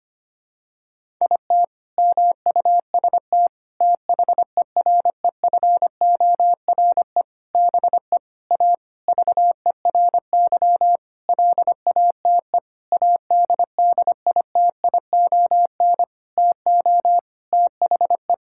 Morse code, heard as a keyed tone, about 25 wpm, 700 Hz. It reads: IT MUST THEREFORE BE A VERY LATE ADDITION TO THE
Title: Sentences from Top 1000 Words - Review Part 6 - 25wpm